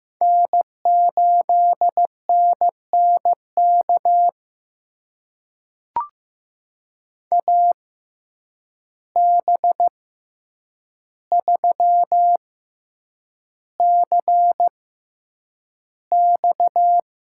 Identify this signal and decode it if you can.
Morse code, heard as a keyed tone, about 15 wpm, spaced as for 8 wpm, 700 Hz. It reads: N8NNK AB3CX
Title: USA Call Signs - 5 Characters - 6x Character Spacing - 15wpm